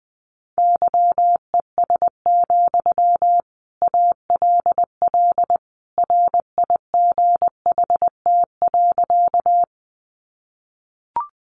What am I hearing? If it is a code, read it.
Morse code, heard as a keyed tone, about 20 wpm, 700 Hz. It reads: YES, ALL RIGHT.
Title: Sentences from Top 500 Words 20wpm